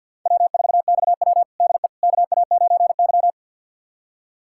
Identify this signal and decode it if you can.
Morse code, heard as a keyed tone, about 53 wpm, 700 Hz. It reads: W4XW DE KA9Y
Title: USA Callsign DE Callsign - 4 Characters - 50wpm